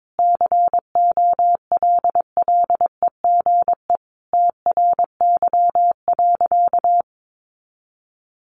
Morse code, heard as a keyed tone, about 22 wpm, 700 Hz. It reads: COLLEGE TRY.
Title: Sentences from Top 1300 Words 22wpm